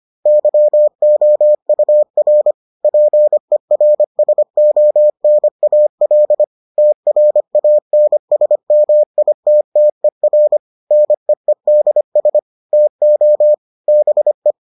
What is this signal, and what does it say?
Morse code, heard as a keyed tone, about 25 wpm, 600 Hz. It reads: YOUR PERSONAL TRANSMITTER NEEDS TO BE